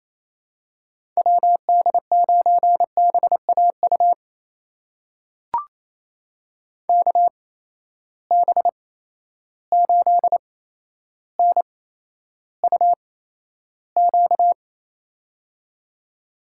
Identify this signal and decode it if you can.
Morse code, heard as a keyed tone, about 28 wpm, spaced as for 12 wpm, 700 Hz. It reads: WD9BAU KB8NUQ
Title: USA Call Signs - 6 Characters - 8x Character Spacing - 28wpm